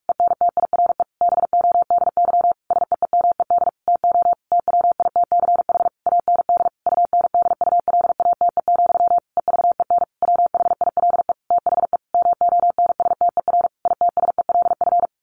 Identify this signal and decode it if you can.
Morse code, heard as a keyed tone, about 46 wpm, 700 Hz. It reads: ENTIRE BODY SEEMED TO TWITCH AND UNDULATE, EVEN WHILE THE MONSTER ITSELF
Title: A Sound Of Thunder 45wpm